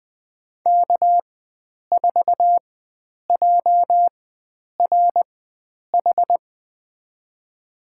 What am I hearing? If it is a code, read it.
Morse code, heard as a keyed tone, about 20 wpm, spaced as for 9 wpm, 700 Hz. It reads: K4JRH
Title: USA Call Signs - 5 Characters - 4x Character Spacing - 20wpm